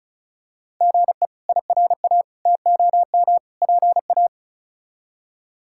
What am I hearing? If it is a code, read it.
Morse code, heard as a keyed tone, about 36 wpm, 700 Hz. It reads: GE IRA TOM PA
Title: K1USN SST QSO Practice - 35wpm